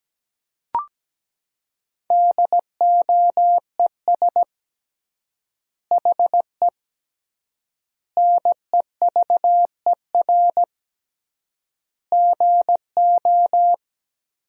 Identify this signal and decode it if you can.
Morse code, heard as a keyed tone, about 17 wpm, 700 Hz. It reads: DOES HE NEVER GO
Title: 3x Word-Spacing - Sentences from Top 300 Words - 17wpm